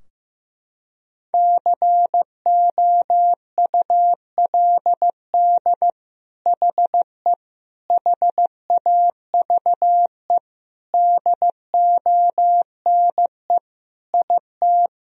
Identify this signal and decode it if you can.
Morse code, heard as a keyed tone, about 15 wpm, 700 Hz. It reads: COULD HE HAVE DONE IT
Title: Sentences from Top 1000 Words - Review Part 3 - 15wpm